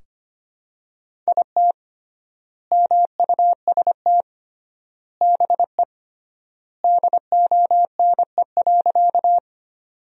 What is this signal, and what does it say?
Morse code, heard as a keyed tone, about 25 wpm, 700 Hz. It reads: IT MUST BE DONE.
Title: 3x Word-Spacing - Sentences from Top 500 Words - 25wpm